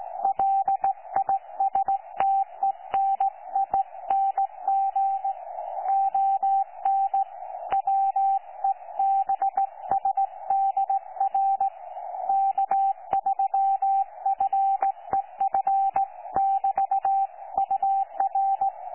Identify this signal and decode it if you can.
Morse code, heard as a keyed tone, about 17 wpm, 780 Hz. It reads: LISTENING ON WEBSDR K3FEF<BT>UR
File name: 12 39Z 18088.2kHz utah aa7fv k9tjl